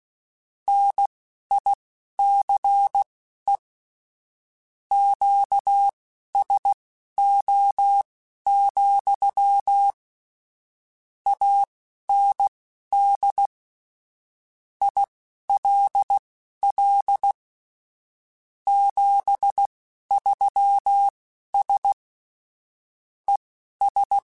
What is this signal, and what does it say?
Morse code, heard as a keyed tone, about 16 wpm, spaced as for 11 wpm, 800 Hz. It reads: NICE QSO, AND ILL 73S ES